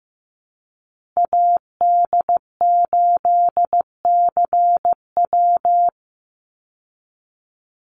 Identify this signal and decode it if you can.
Morse code, heard as a keyed tone, about 15 wpm, 700 Hz. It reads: AD8CW
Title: Call Signs Repeated 3x - 15wpm